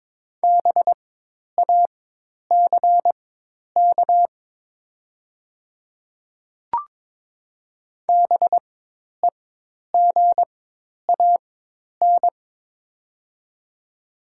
Morse code, heard as a keyed tone, about 22 wpm, spaced as for 10 wpm, 700 Hz. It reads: BACK BEGAN
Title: 4x Character Spacing - Top 300 Words - 22wpm